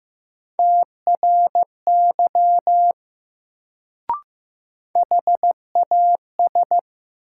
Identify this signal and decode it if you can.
Morse code, heard as a keyed tone, about 15 wpm, 700 Hz. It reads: TRY HAS
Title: Sets of 2 Words - New Word Drill - 15wpm